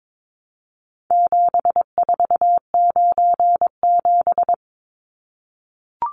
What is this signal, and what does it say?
Morse code, heard as a keyed tone, about 22 wpm, 700 Hz. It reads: 7497
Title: Numbers - 4-Digit 22wpm